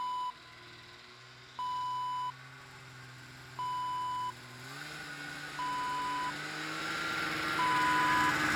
There is no code.